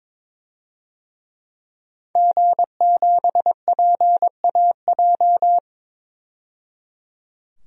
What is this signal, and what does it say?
Morse code, heard as a keyed tone, about 21 wpm, 700 Hz. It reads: G7PAJ
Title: DXCC-100 Call Signs - 5 Characters - 22wpm